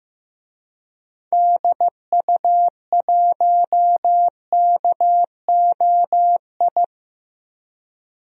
Morse code, heard as a keyed tone, about 15 wpm, 700 Hz. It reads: DU1KOI
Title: DXCC-100 Call Signs - 6 Characters - 15wpm